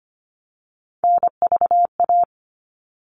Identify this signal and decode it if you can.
Morse code, heard as a keyed tone, about 25 wpm, 700 Hz. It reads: NVA
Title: All Cut Numbers - 3-Digit 25wpm